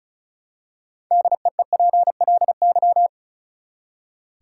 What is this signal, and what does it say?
Morse code, heard as a keyed tone, about 34 wpm, 700 Hz. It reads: DEEPLY
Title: Six Letter Words - 35wpm